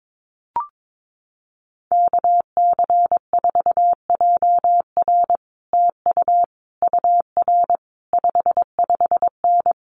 Morse code, heard as a keyed tone, about 22 wpm, 700 Hz. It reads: KC4JR TU UR 55N
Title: Common QSO Phrases Encore 22wpm